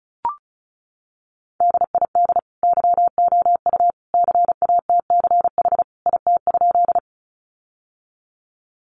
Morse code, heard as a keyed tone, about 34 wpm, 700 Hz. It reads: DID YOU CATCH IT?
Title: Sentences from Top 1500 Words 35wpm